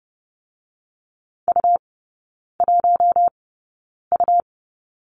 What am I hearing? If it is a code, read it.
Morse code, heard as a keyed tone, about 30 wpm, spaced as for 9 wpm, 690 Hz. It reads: U1U
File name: Lesson 17e - Letter u - Three Characters 30wpm